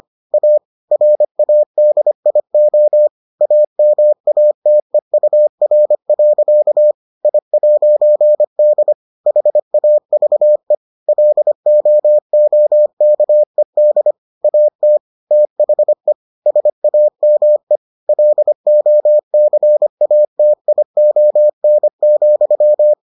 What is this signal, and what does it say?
Morse code, heard as a keyed tone, about 25 wpm, 600 Hz. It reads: A RADIO AMATEUR. I'D HAVE LOOKED AT THE SAME LOCATION,